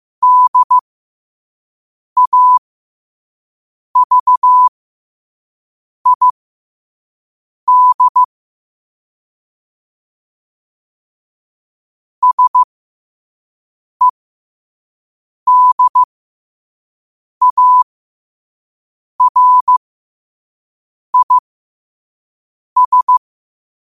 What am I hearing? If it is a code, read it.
Morse code, heard as a keyed tone, about 15 wpm, spaced as for 5 wpm, 1000 Hz. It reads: DAVID SEDARIS
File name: qotdfiles / 5wpm